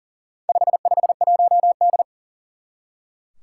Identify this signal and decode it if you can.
Morse code, heard as a keyed tone, about 38 wpm, 700 Hz. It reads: 551D
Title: Four Challenging Letter-Number Combinations 40wpm